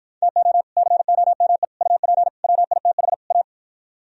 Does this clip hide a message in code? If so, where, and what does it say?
Yes: Morse code from 0.2 to 3.4 s, heard as a keyed tone, about 53 wpm, 700 Hz. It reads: TO COME UP WITH A